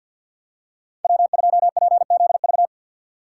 Morse code, heard as a keyed tone, about 51 wpm, 690 Hz. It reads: W2PZV
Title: USA Call Signs - 5 Characters - 3x Character Spacing - 50wpm